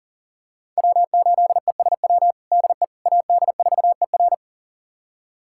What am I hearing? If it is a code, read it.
Morse code, heard as a keyed tone, about 40 wpm, 700 Hz. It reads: W8ESW DE AD4ER
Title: USA Callsign DE Callsign - 5 Characters - 40wpm